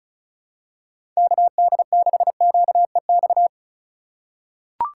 Morse code, heard as a keyed tone, about 34 wpm, 700 Hz. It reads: KD6QEX
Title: USA Call Signs - 3x Character Spacing - 35wpm